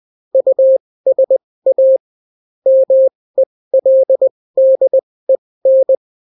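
Morse code, heard as a keyed tone, about 20 wpm, spaced as for 16 wpm, 530 Hz. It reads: USA MELDEN
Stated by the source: Daily Morse weather and top stories at 16 wpm, clean without QRN.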